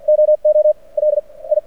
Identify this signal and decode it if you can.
Morse code, heard as a keyed tone, about 51 wpm, spaced as for 28 wpm, 610 Hz. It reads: OOLI
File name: archived / CRSnet